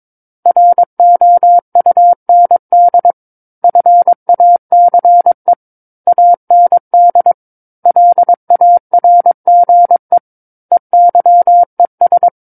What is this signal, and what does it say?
Morse code, heard as a keyed tone, about 22 wpm, 700 Hz. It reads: ROUND FACE AND LARGE EYES
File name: 06-Chapter-06 22 WPM